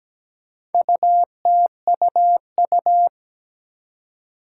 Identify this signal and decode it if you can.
Morse code, heard as a keyed tone, about 17 wpm, 700 Hz. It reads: UTUU
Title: Four Challenging Letter-Number Combinations 17wpm